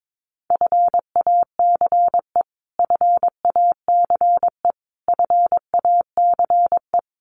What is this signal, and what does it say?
Morse code, heard as a keyed tone, about 22 wpm, 700 Hz. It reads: FACE FACE FACE
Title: Top 500 Words Repeated 22wpm